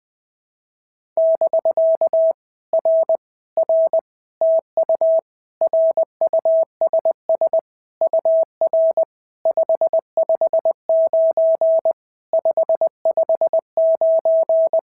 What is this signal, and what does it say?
Morse code, heard as a keyed tone, about 20 wpm, 650 Hz. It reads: <BK> R R TU RUSS UR 559 559
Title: POTA QSO Practice - 20wpm